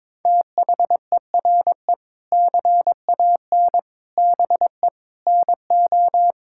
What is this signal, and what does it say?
Morse code, heard as a keyed tone, about 22 wpm, 700 Hz. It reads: THERE CAN BE NO
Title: Sentences from Top 1000 Words - Review Part 5 - 22wpm